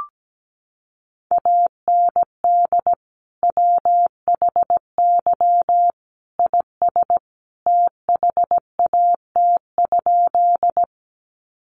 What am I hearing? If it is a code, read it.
Morse code, heard as a keyed tone, about 17 wpm, 700 Hz. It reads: AND WHY IS THAT?
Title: Sentences from Top 200 Words 17wpm